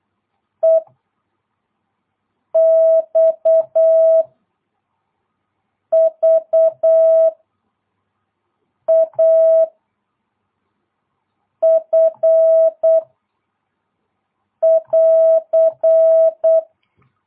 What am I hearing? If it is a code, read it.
Morse code, heard as a keyed tone, about 8 wpm, 650 Hz. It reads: E X V A F <AR>
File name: jh8cbh / 2022cw